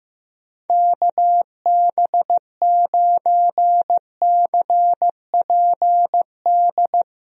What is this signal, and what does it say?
Morse code, heard as a keyed tone, about 15 wpm, 700 Hz. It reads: KB9CPD
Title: USA Call Signs - 6 Characters - Repeated 3x - 15wpm